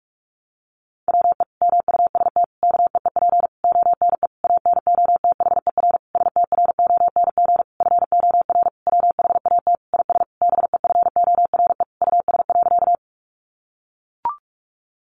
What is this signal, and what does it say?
Morse code, heard as a keyed tone, about 44 wpm, 700 Hz. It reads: WE MUST KEEP ONE ANOTHER STRONG FOR WHAT IS BEFORE US.
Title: Sentences from Top 600 Words 45wpm